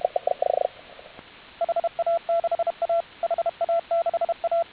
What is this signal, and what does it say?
Morse code, heard as a keyed tone, about 32 wpm, 1350 Hz, with radio noise. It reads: HA6A HA6A